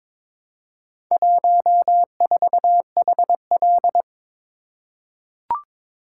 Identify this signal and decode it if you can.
Morse code, heard as a keyed tone, about 22 wpm, 700 Hz. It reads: 14HL
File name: practice / 22wpm / Four Letter-Number Combinations - 22wpm